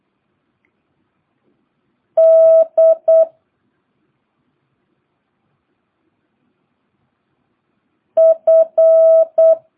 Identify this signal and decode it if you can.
Morse code, heard as a keyed tone, about 8 wpm, 650 Hz. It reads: D F
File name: jh8cbh / 2022cw